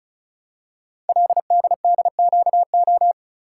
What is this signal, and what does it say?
Morse code, heard as a keyed tone, about 36 wpm, 700 Hz. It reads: LDDQO
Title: Code Groups - 5 Letters - 35wpm